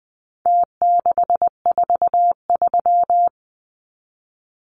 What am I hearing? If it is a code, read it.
Morse code, heard as a keyed tone, about 20 wpm, 700 Hz. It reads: T643
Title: Numbers - Contest Style - 4-Digit 20wpm